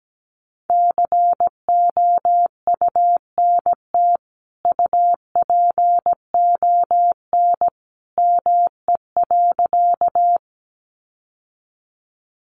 Morse code, heard as a keyed tone, about 17 wpm, 700 Hz. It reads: COUNT UPON ME.
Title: Sentences from Top 1500 Words 17wpm